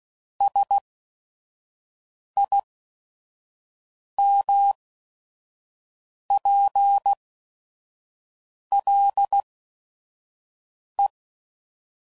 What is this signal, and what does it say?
Morse code, heard as a keyed tone, about 16 wpm, spaced as for 5 wpm, 800 Hz. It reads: SIMPLE
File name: CW Code Groups @5WPM / Technical Subjects / AND Gate